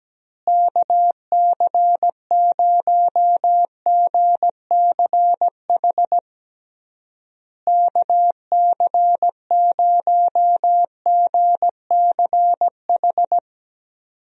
Morse code, heard as a keyed tone, about 17 wpm, 700 Hz. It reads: KC0GCH KC0GCH
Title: USA Call Signs - 6 Characters - Repeated 3x - 17wpm